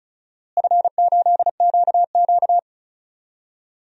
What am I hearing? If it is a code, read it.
Morse code, heard as a keyed tone, about 36 wpm, 700 Hz. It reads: F8QQ
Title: DXCC-100 Call Signs - 4 Characters - Repeated 3x - 35wpm